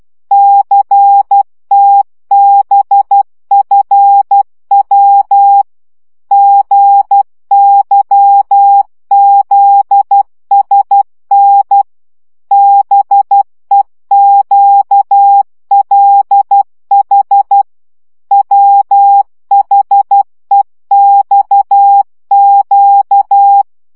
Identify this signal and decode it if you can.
Morse code, heard as a keyed tone, about 12 wpm, 800 Hz. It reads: CTBFW GYZSN BEQLH WHEXQ
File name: morze vizsga 2011-2 10